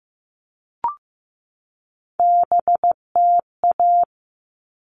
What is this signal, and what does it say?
Morse code, heard as a keyed tone, about 15 wpm, 700 Hz. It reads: BTA
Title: All Cut Numbers - 3-Digit 15wpm